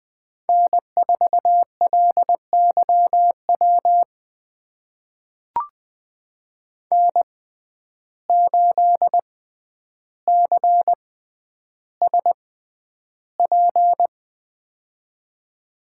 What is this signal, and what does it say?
Morse code, heard as a keyed tone, about 20 wpm, spaced as for 10 wpm, 700 Hz. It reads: N4LYW N8CSP
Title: USA Call Signs - 5 Characters - 6x Character Spacing - 20wpm